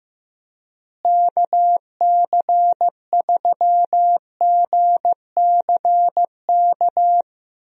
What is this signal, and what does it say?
Morse code, heard as a keyed tone, about 15 wpm, 700 Hz. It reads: KC3GCK
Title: USA Call Signs - 6 Characters - Repeated 3x - 15wpm